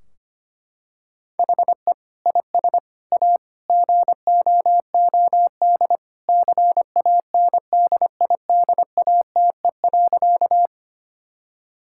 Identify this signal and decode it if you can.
Morse code, heard as a keyed tone, about 25 wpm, 700 Hz. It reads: HE IS A GOOD CANDIDATE.
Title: Sentences from Top 1600 Words 25wpm